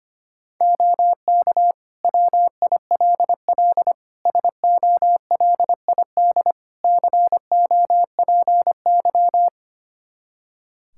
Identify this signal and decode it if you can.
Morse code, heard as a keyed tone, about 25 wpm, 700 Hz. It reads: OK WILL SOLID COPY
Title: Practice QSO - Exchange 3 - More Info - 25wpm